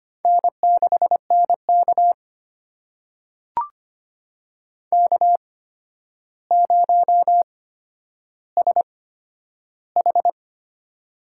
Morse code, heard as a keyed tone, about 25 wpm, spaced as for 11 wpm, 700 Hz. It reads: N6NK K0SH
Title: USA Call Signs - 4 Characters - 8x Character Spacing - 25wpm